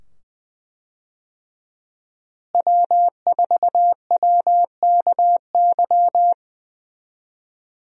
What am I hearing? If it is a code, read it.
Morse code, heard as a keyed tone, about 20 wpm, 700 Hz. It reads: W4WKY
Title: USA Call Signs - 5 Characters - Repeated 2x - 20wpm